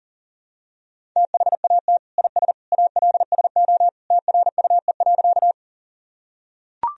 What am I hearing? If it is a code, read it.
Morse code, heard as a keyed tone, about 37 wpm, 700 Hz. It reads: THAT IS ALSO TRUE.
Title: Sentences from Top 1000 Words - Review Part 1 - 40wpm